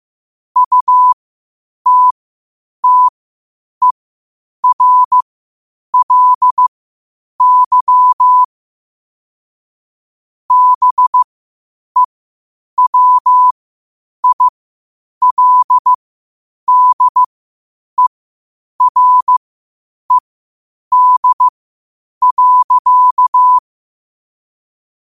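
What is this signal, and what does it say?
Morse code, heard as a keyed tone, about 15 wpm, spaced as for 8 wpm, 1000 Hz. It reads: UTTERLY BEWILDERED.
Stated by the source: Morse code quotes at 8 WPM.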